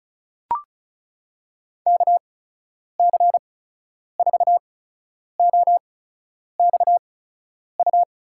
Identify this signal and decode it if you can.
Morse code, heard as a keyed tone, about 34 wpm, spaced as for 10 wpm, 700 Hz. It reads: KC4OXU
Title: USA Call Signs - 6 Characters - 8x Character Spacing - 35wpm